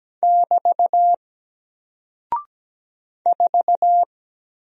Morse code, heard as a keyed tone, about 17 wpm, 700 Hz. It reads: <BT> 4
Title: Single Character (Letter:Number:Punctuation:Prosign) 17wpm